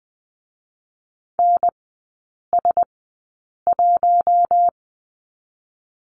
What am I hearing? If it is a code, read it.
Morse code, heard as a keyed tone, about 20 wpm, spaced as for 8 wpm, 700 Hz. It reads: NS1
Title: 060 Lesson 11e - Letter h - Three Characters 20wpm